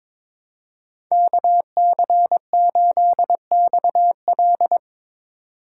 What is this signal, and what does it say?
Morse code, heard as a keyed tone, about 23 wpm, 700 Hz. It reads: KC8XL
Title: Code Groups - 5 Letter:Numbers - 22wpm